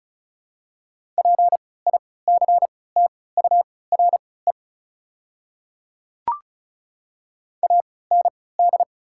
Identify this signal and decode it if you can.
Morse code, heard as a keyed tone, about 34 wpm, spaced as for 20 wpm, 700 Hz. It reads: PICTURE AND
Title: 3x Character Spacing - Top 300 Words - 35wpm